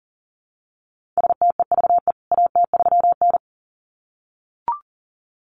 Morse code, heard as a keyed tone, about 40 wpm, 700 Hz. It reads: STEVE AT3N
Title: CWT Exchange 40wpm